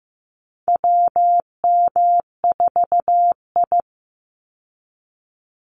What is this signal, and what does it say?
Morse code, heard as a keyed tone, about 15 wpm, 700 Hz. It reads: WM4I
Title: Call Signs Repeated 3x - 15wpm